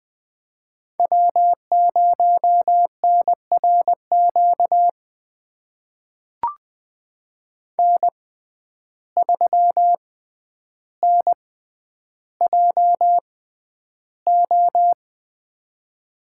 Morse code, heard as a keyed tone, about 20 wpm, spaced as for 10 wpm, 700 Hz. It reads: W0NRQ N3NJO
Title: USA Call Signs - 5 Characters - 6x Character Spacing - 20wpm